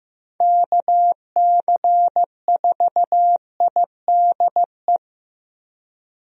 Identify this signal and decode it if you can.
Morse code, heard as a keyed tone, about 15 wpm, 700 Hz. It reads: KC4IDE
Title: USA Call Signs - 6 Characters - Repeated 2x - 15wpm